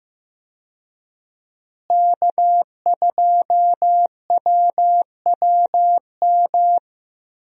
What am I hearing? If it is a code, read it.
Morse code, heard as a keyed tone, about 15 wpm, 700 Hz. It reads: K2WWM
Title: USA Call Signs - 5 Characters - Repeated 2x - 15wpm